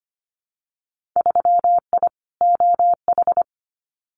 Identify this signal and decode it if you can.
Morse code, heard as a keyed tone, about 25 wpm, 700 Hz. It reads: 3I OH
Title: Winter Field Day Exchanges Encore 25wpm